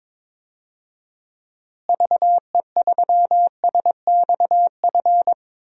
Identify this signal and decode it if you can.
Morse code, heard as a keyed tone, about 22 wpm, 700 Hz. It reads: VE3SXF
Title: DXCC-100 Call Signs - 6 Characters - Repeated 2x - 22wpm